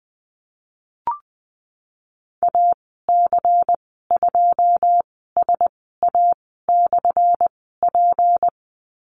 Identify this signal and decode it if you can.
Morse code, heard as a keyed tone, about 19 wpm, spaced as for 14 wpm, 700 Hz. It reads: AC2SA/P
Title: 179 Lesson 29g - Slash - Call Signs 20wpm